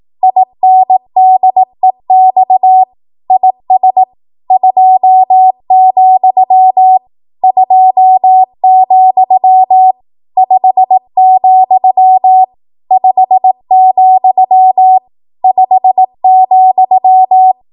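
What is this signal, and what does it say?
Morse code, heard as a keyed tone, about 18 wpm, 750 Hz. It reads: INDEX IS 2, 2, 5, 5, 5,
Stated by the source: ARRL 18 wpm Morse code practice archive transmission.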